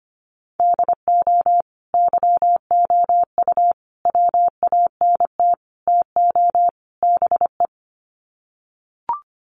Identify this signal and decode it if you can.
Morse code, heard as a keyed tone, about 25 wpm, 700 Hz. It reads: DO YOU WANT TO BE
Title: Sets of 5 Words Encore2 25wpm